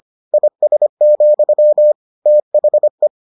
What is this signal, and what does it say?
Morse code, heard as a keyed tone, about 25 wpm, 600 Hz. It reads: IS, THE